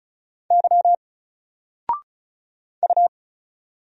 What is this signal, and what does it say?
Morse code, heard as a keyed tone, about 36 wpm, 700 Hz. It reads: Y U
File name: Single-Letter-Number-35wpm